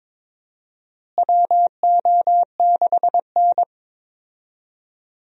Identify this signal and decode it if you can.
Morse code, heard as a keyed tone, about 23 wpm, 700 Hz. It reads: WO6N
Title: USA Call Signs - 4 Characters - Repeated 2x - 22wpm